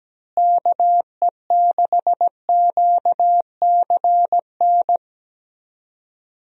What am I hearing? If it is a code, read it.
Morse code, heard as a keyed tone, about 17 wpm, 700 Hz. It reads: KE6QCN
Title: USA Call Signs - 6 Characters - Repeated 2x - 17wpm